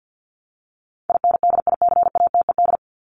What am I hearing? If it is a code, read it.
Morse code, heard as a keyed tone, about 50 wpm, 700 Hz. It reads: INDICATED
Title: ED Suffix - Top 100 Words with Base Words 50wpm